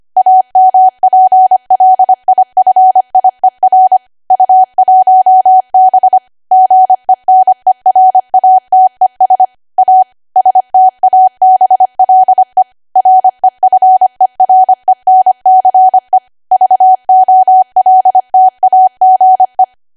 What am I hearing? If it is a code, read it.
Morse code, heard as a keyed tone, about 25 wpm, 750 Hz. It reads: AMPLIFIER U1B GENERATES A STABLE REFERENCE VOLTAGE
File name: Archive / 25 WPM / 25WPM